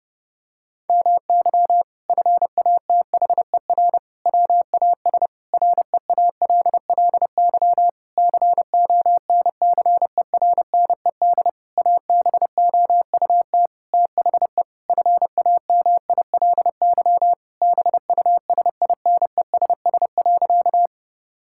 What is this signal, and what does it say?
Morse code, heard as a keyed tone, about 30 wpm, 700 Hz. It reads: MY FATHER WAS REALLY CONCERNED ABOUT THE FAMILY BUSINESS.
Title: Sentences from Top 1000 Words - Review Part 6 - 30wpm